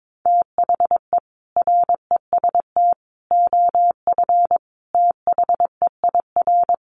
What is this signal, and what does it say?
Morse code, heard as a keyed tone, about 22 wpm, 700 Hz. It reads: THE REST OF THEIR
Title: Sets of 5 Words Encore 22wpm